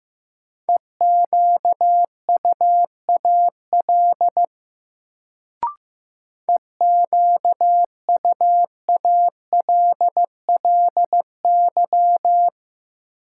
Morse code, heard as a keyed tone, about 15 wpm, 700 Hz. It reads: EQUAL EQUALLY
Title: Sentences from Top 1400 Words - New Word Drill - 15wpm